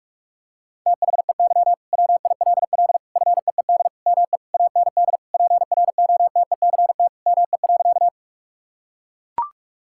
Morse code, heard as a keyed tone, about 41 wpm, 700 Hz. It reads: THEY WILL FEED ME AND PROTECT ME.
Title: Sentences from Top 1600 Words 45wpm